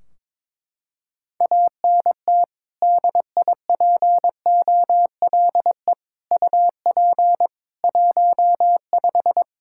Morse code, heard as a keyed tone, about 22 wpm, 700 Hz. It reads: ANT DIPOLE UP 15
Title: Practice QSO - Exchange 2 - Info - 22wpm